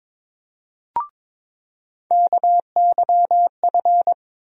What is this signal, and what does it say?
Morse code, heard as a keyed tone, about 22 wpm, 700 Hz. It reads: KYF
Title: USA Call Sign Postfixes - 3 Letters 22wpm